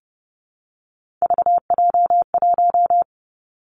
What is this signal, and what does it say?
Morse code, heard as a keyed tone, about 30 wpm, 700 Hz. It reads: VJ1
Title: Three Challenging Letter-Number Combinations 30wpm